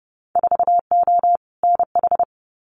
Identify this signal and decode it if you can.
Morse code, heard as a keyed tone, about 32 wpm, 700 Hz. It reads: 4O NH
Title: Winter Field Day Exchanges Encore 30wpm